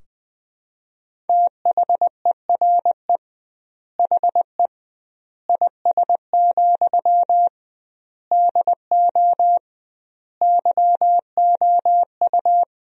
Morse code, heard as a keyed tone, about 20 wpm, 700 Hz. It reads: THERE HE IS, DO YOU
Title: 2x Word-Spacing - Sentences from Top 100 Words - 20wpm